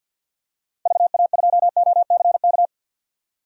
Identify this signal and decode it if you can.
Morse code, heard as a keyed tone, about 52 wpm, 700 Hz. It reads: VA2OCK
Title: DXCC-100 Call Signs - 6 Characters - 50wpm